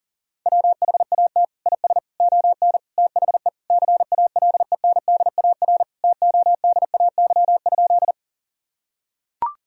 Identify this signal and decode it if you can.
Morse code, heard as a keyed tone, about 40 wpm, 700 Hz. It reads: WHAT IS ON THE CALENDAR TODAY?
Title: Sentences from Top 1600 Words 40wpm